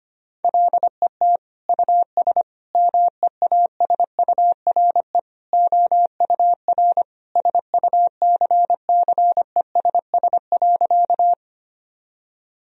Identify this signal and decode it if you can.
Morse code, heard as a keyed tone, about 25 wpm, 700 Hz. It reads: LET US MEASURE OUR SUCCESS.
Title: Sentences from Top 1000 Words - Review Part 2 - 25wpm